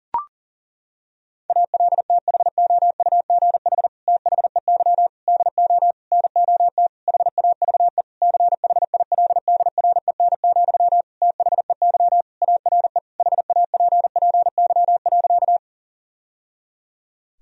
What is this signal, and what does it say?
Morse code, heard as a keyed tone, about 40 wpm, 700 Hz. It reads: ALTHOUGH THEY DO NOT HAVE CHILDREN, THEY ARE HAPPY.
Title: Sentences from Top 1000 Words - Review Part 3 - 40wpm